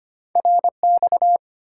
Morse code, heard as a keyed tone, about 25 wpm, 700 Hz. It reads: RX
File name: Top 90 QSO Elements - 8x Character Spacing - 25wpm